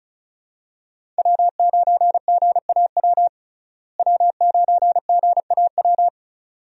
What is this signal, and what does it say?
Morse code, heard as a keyed tone, about 34 wpm, 700 Hz. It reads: W9GAW W9GAW
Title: USA Call Signs - 5 Characters - Repeated 2x - 35wpm